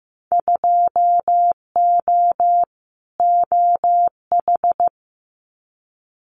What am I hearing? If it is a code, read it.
Morse code, heard as a keyed tone, about 15 wpm, 700 Hz. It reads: 2O OH
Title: Winter Field Day Exchanges 15wpm